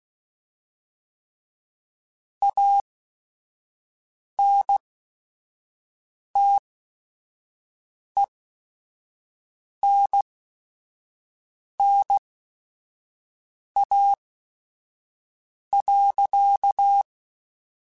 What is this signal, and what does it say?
Morse code, heard as a keyed tone, about 16 wpm, spaced as for 5 wpm, 800 Hz. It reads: ANTENNA.